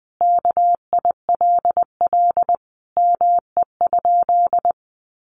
Morse code, heard as a keyed tone, about 19 wpm, 700 Hz. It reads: KILL ME?
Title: Sentences from Top 1400 Words 20wpm